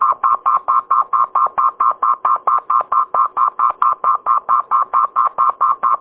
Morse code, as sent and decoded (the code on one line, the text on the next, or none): none